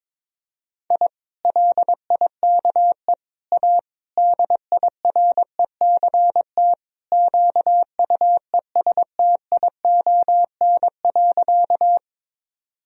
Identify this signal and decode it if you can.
Morse code, heard as a keyed tone, about 22 wpm, 700 Hz. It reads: I LIKE A DIRECT QUESTION.
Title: Sentences from Top 1000 Words - Review Part 2 - 22wpm